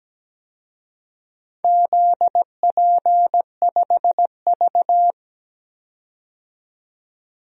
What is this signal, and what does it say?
Morse code, heard as a keyed tone, about 17 wpm, 700 Hz. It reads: ZP5V
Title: DXCC-100 Call Signs - 4 Characters - 17wpm